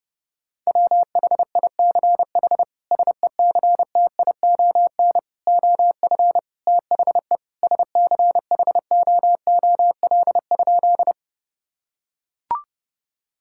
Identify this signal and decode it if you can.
Morse code, heard as a keyed tone, about 32 wpm, 700 Hz. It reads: WHICH SECTION OF THE SCHOOL?
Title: Sentences from Top 1000 Words - Review Part 5 - 30wpm